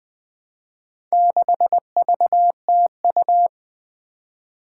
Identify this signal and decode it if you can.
Morse code, heard as a keyed tone, about 20 wpm, 700 Hz. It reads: 6VTU
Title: Four Challenging Letter-Number Combinations 20wpm